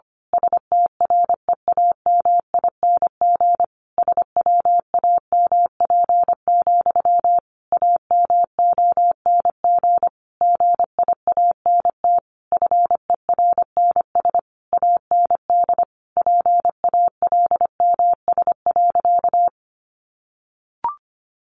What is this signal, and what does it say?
Morse code, heard as a keyed tone, about 25 wpm, 700 Hz. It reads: STREAMING SWAMP, AMONG GIANT FERNS AND PALMS.
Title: A Sound Of Thunder 25wpm